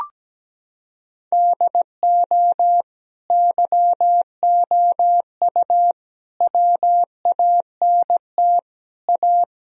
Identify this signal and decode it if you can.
Morse code, heard as a keyed tone, about 17 wpm, 700 Hz. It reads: DO YOU WANT A
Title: Sentences from Top 1000 Words - Review Part 2 - 17wpm